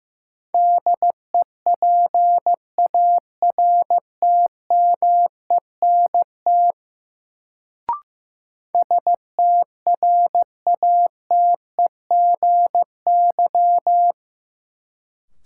Morse code, heard as a keyed tone, about 15 wpm, 700 Hz. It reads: DEPARTMENT STRATEGY
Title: Sentences from Top 1000 Words - Review Part 5 - New Word Drill - 15wpm